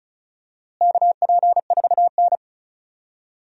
Morse code, heard as a keyed tone, about 34 wpm, 700 Hz. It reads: KP4N
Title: DXCC-100 Call Signs - 4 Characters - Repeated 3x - 35wpm